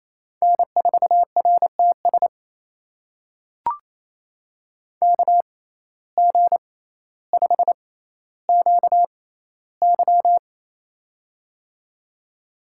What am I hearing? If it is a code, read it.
Morse code, heard as a keyed tone, about 28 wpm, spaced as for 14 wpm, 700 Hz. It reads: N4RTS KG5QY
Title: USA Call Signs - 5 Characters - 6x Character Spacing - 28wpm